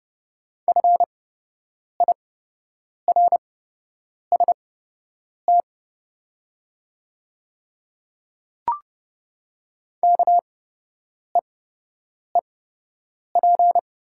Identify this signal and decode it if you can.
Morse code, heard as a keyed tone, about 30 wpm, spaced as for 8 wpm, 700 Hz. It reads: FIRST KEEP
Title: 8x Character Spacing - Top 300 Words - 30wpm